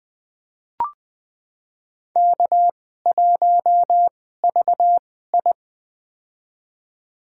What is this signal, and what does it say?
Morse code, heard as a keyed tone, about 20 wpm, spaced as for 15 wpm, 700 Hz. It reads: K1VI
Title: USA Call Signs - 4 Characters - 2x Character Spacing - 20wpm